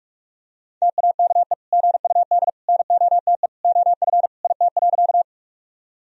Morse code, heard as a keyed tone, about 46 wpm, 690 Hz. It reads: TAKE GUD NOTE OF IT.
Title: Sentences from Top 500 Words and Abbreviations 45wpm